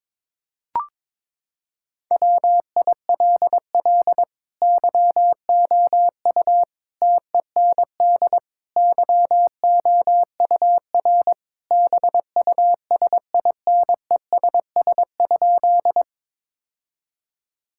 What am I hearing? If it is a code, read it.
Morse code, heard as a keyed tone, about 22 wpm, 700 Hz. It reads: WILL YOU TEND YOUR BUSINESS?
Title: Sentences from Top 1000 Words - Review Part 5 - 22wpm